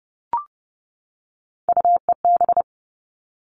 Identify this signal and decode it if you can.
Morse code, heard as a keyed tone, about 30 wpm, 700 Hz. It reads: UEB
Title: All Cut Numbers - 3-Digit 30wpm